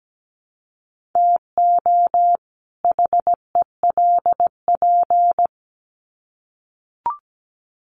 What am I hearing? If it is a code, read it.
Morse code, heard as a keyed tone, about 17 wpm, 700 Hz. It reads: TO HELP
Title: Sets of 2 Words 17wpm